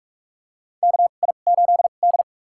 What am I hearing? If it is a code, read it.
Morse code, heard as a keyed tone, about 43 wpm, spaced as for 31 wpm, 700 Hz. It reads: KI8D
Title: USA Call Signs - 4 Characters - 2x Character Spacing - 45wpm